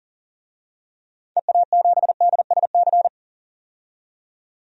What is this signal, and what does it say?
Morse code, heard as a keyed tone, about 40 wpm, 700 Hz. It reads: EA7DSC
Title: DXCC-100 Call Signs - 6 Characters - 40wpm